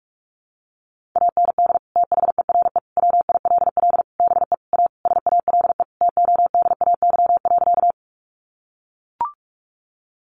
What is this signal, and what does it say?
Morse code, heard as a keyed tone, about 44 wpm, 700 Hz. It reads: AND THERE WILL BE A SALE TODAY.
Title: Sentences from Top 1400 Words 45wpm